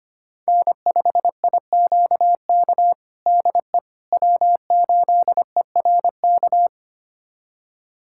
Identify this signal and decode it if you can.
Morse code, heard as a keyed tone, about 25 wpm, 700 Hz. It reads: N5IQK DE W8ERK
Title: USA Callsign DE Callsign - 5 Characters - 25wpm